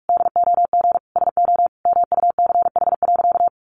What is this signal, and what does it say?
Morse code, heard as a keyed tone, about 44 wpm, 700 Hz. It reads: DOG SO MUCH.
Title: Additional Easy Phrases 45wpm